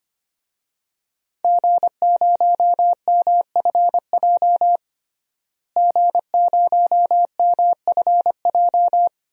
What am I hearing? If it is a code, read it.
Morse code, heard as a keyed tone, about 25 wpm, 700 Hz. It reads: G0MFJ G0MFJ
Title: DXCC-100 Call Signs - 5 Characters - Repeated 2x - 25wpm